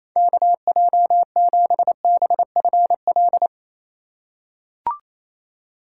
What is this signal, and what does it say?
Morse code, heard as a keyed tone, about 27 wpm, 700 Hz. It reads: KJ7BFL
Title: USA Call Signs - 6 Characters - 2x Character Spacing - 28wpm